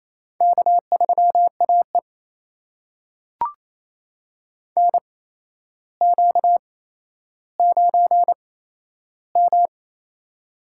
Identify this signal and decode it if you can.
Morse code, heard as a keyed tone, about 28 wpm, spaced as for 12 wpm, 700 Hz. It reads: K3AE NQ9M
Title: USA Call Signs - 8x Character Spacing - 28wpm